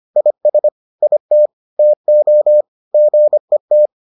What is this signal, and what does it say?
Morse code, heard as a keyed tone, about 25 wpm, 600 Hz. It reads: IS IT TO GET